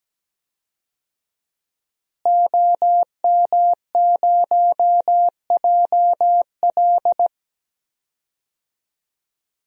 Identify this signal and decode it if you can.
Morse code, heard as a keyed tone, about 17 wpm, 700 Hz. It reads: OM0JL
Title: DXCC-100 Call Signs - 5 Characters - Repeated 3x - 17wpm